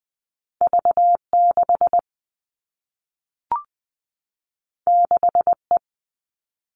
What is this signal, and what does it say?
Morse code, heard as a keyed tone, about 20 wpm, 700 Hz. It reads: V 6 6 E
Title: Two Challenging Letter-Number Combinations 20wpm